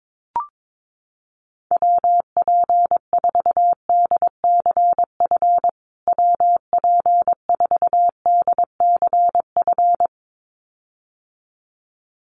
Morse code, heard as a keyed tone, about 22 wpm, 700 Hz. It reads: WP4DCF WP4DCF
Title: USA Call Signs Repeated 2x 22wpm